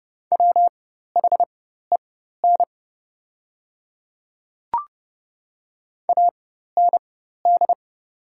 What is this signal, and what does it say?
Morse code, heard as a keyed tone, about 30 wpm, spaced as for 14 wpm, 700 Hz. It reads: WHEN AND
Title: 4x Character Spacing - Top 100 Words - 30wpm